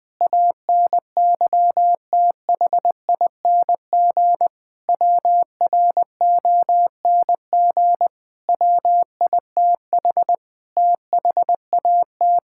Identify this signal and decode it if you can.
Morse code, heard as a keyed tone, about 20 wpm, 700 Hz. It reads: ANYTHING WRONG WITH THAT
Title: Sentences with 7-Word Phrases - 20wpm